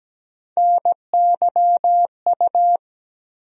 Morse code, heard as a keyed tone, about 17 wpm, 700 Hz. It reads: NYU
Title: USA Call Sign Postfixes - 3 Letters 17wpm